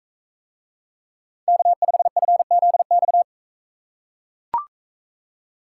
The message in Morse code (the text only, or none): K5FZX